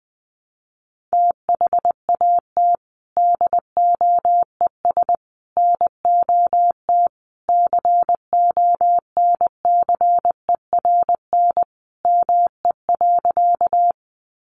Morse code, heard as a keyed tone, about 20 wpm, 700 Hz. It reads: THAT DOES NOT CONCERN ME.
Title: Sentences from Top 1000 Words 20wpm